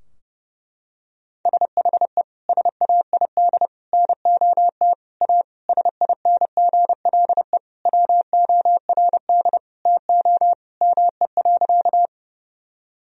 Morse code, heard as a keyed tone, about 32 wpm, 700 Hz. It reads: SHE SAID NOT A SINGLE WORD TO ME.
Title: Sentences from Top 1000 Words - Review Part 4 - 30wpm